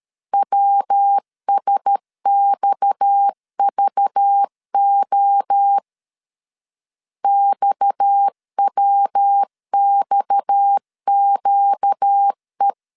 Morse code, heard as a keyed tone, about 13 wpm, 800 Hz. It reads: WSXVO XWXQE
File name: LEZIONI / 6 PKX0 / 6 PKX 12wpm 2024